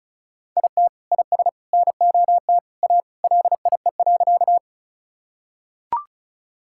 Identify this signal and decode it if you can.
Morse code, heard as a keyed tone, about 36 wpm, 700 Hz. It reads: IT IS NOT A LIE.
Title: Sentences from Top 1600 Words 35wpm